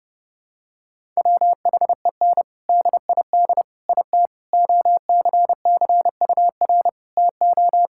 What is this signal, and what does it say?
Morse code, heard as a keyed tone, about 32 wpm, 700 Hz. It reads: WHEN DID IT OCCUR TO
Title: Sentences from Top 1000 Words - Review Part 6 - 30wpm